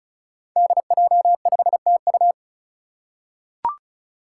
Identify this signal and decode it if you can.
Morse code, heard as a keyed tone, about 36 wpm, 700 Hz. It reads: DJ5TU
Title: Five Challenging Letter-Number Combinations 35wpm